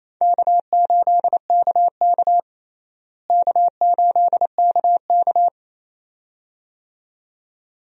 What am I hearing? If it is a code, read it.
Morse code, heard as a keyed tone, about 28 wpm, 700 Hz. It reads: K8KK K8KK
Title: USA Call Signs - 4 Characters - Repeated 2x - 28wpm